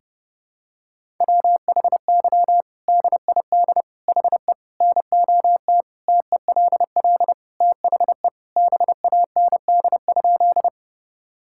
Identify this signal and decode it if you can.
Morse code, heard as a keyed tone, about 30 wpm, 700 Hz. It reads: WHY DID HE NOT TELL THE BAND?
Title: Sentences from Top 1600 Words 30wpm